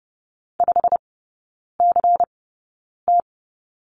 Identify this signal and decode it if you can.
Morse code, heard as a keyed tone, about 30 wpm, spaced as for 9 wpm, 700 Hz. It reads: 5CT
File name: Lesson 19e - Letter w - Three Characters 30wpm